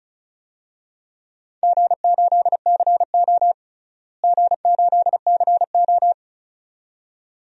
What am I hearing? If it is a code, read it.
Morse code, heard as a keyed tone, about 34 wpm, 700 Hz. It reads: G8CO G8CO
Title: DXCC-100 Call Signs - 4 Characters - Repeated 2x - 35wpm